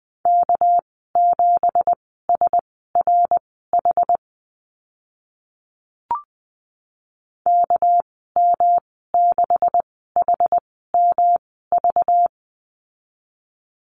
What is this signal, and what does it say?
Morse code, heard as a keyed tone, about 20 wpm, spaced as for 14 wpm, 700 Hz. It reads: K7SRH KM6HMV
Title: 231 Lesson 37g - Letter x - Call Signs 20wpm